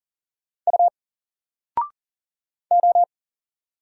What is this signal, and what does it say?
Morse code, heard as a keyed tone, about 40 wpm, 700 Hz. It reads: U O
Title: Single Letter-Number 40wpm